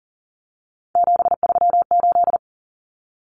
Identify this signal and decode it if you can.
Morse code, heard as a keyed tone, about 40 wpm, 700 Hz. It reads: 738
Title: Numbers - 3-Digit 40wpm